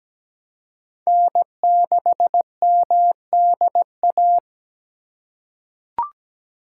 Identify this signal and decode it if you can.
Morse code, heard as a keyed tone, about 17 wpm, 700 Hz. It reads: N6MDA
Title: USA Call Signs - 5 Characters - 3x Character Spacing - 17wpm